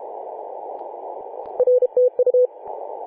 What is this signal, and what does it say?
Morse code, heard as a keyed tone, about 32 wpm, 500 Hz. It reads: R TU
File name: 2013/Sep/21 2035z 3D2GC 18.072MHz CW